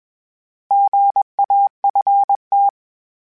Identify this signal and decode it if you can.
Morse code, heard as a keyed tone, about 21 wpm, 800 Hz. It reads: GAFT
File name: Misc / Misc-mp3s / Misc @20